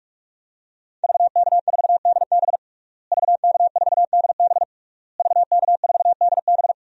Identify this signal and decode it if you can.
Morse code, heard as a keyed tone, about 46 wpm, 700 Hz. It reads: VK4DB VK4DB VK4DB
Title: DXCC-100 Call Signs - 5 Characters - Repeated 3x - 45wpm